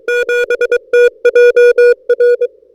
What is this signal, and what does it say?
Morse code, heard as a keyed tone, about 22 wpm, 480 Hz. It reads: 7TJR
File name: n6tr / A45XR-K7TJR-14Dec2012